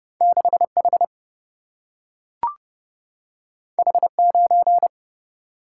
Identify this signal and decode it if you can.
Morse code, heard as a keyed tone, about 30 wpm, 700 Hz. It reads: 6 H H 9
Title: Single-Letter Single-Number 30wpm